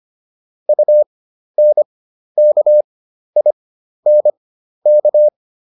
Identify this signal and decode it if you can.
Morse code, heard as a keyed tone, about 25 wpm, spaced as for 12 wpm, 600 Hz. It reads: UNKINK
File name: UN6 12